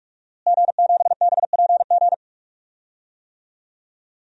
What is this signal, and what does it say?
Morse code, heard as a keyed tone, about 46 wpm, 700 Hz. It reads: G7BPG
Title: DXCC-100 Call Signs - 5 Characters - Repeated 2x - 45wpm